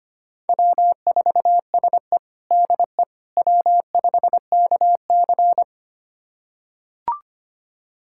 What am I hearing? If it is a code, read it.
Morse code, heard as a keyed tone, about 25 wpm, 700 Hz. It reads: W4SE DE W5KC
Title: USA Callsign DE Callsign - 4 Characters - 25wpm